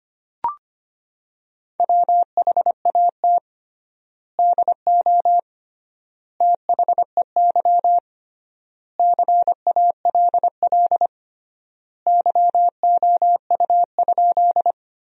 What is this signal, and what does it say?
Morse code, heard as a keyed tone, about 25 wpm, 700 Hz. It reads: WHAT DO THEY CALL YOU?
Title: 3x Word-Spacing - Sentences from Top 100 Words - 25wpm